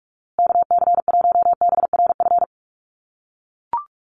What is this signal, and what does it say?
Morse code, heard as a keyed tone, about 44 wpm, 700 Hz. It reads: KC1BRF
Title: USA Call Signs Repeated 2x 45wpm